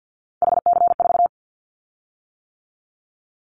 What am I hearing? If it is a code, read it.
Morse code, heard as a keyed tone, about 50 wpm, 700 Hz. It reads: HC4
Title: Callsign Prefixes for Top 100 DXCC - 50wpm